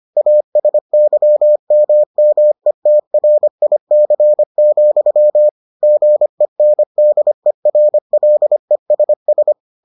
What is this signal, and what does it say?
Morse code, heard as a keyed tone, about 25 wpm, 600 Hz. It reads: ASYMMETRIC, GENDERLESS